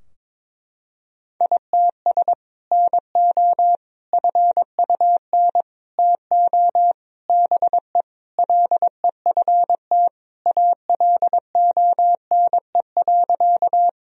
Morse code, heard as a keyed tone, about 22 wpm, 700 Hz. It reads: ITS NO FUN TO BE LEFT ALONE.
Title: Sentences from Top 1000 Words - Review Part 4 - 22wpm